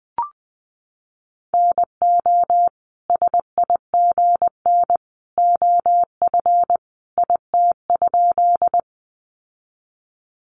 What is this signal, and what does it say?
Morse code, heard as a keyed tone, about 20 wpm, 700 Hz. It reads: NO SIGN OF IT?
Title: Sentences from Top 1000 Words 20wpm